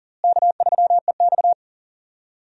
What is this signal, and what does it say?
Morse code, heard as a keyed tone, about 40 wpm, 700 Hz. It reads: K3EX
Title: USA Call Signs - 4 Characters - Repeated 3x - 40wpm